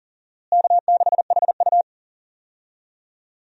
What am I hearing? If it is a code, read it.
Morse code, heard as a keyed tone, about 40 wpm, 700 Hz. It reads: K6HU
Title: USA Call Signs - 4 Characters - Repeated 3x - 40wpm